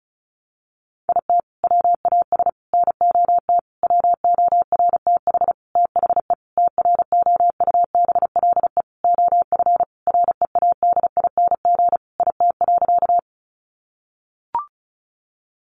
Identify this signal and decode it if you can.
Morse code, heard as a keyed tone, about 34 wpm, 700 Hz. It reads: IT WAS NOT WORTH THE TROUBLE OF READING IT.
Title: Sentences from Top 1200 Words 35wpm